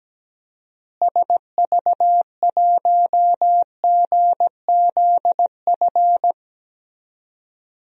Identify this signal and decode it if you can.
Morse code, heard as a keyed tone, about 17 wpm, 700 Hz. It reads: SV1GZF
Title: DXCC-100 Call Signs - 6 Characters - Repeated 3x - 17wpm